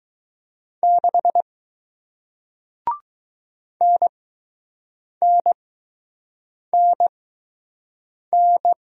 Morse code, heard as a keyed tone, about 20 wpm, spaced as for 6 wpm, 700 Hz. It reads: 6 NNNN